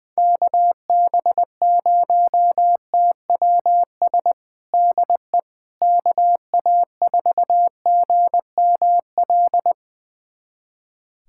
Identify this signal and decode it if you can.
Morse code, heard as a keyed tone, about 20 wpm, 700 Hz. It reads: KB0TWS DE KA4GML
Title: USA Callsign DE Callsign - 6 Characters - 20wpm